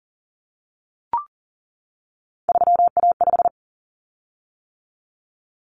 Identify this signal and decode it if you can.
Morse code, heard as a keyed tone, about 40 wpm, 700 Hz. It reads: 3A5
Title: Common Cut Numbers - 3-Digit 40wpm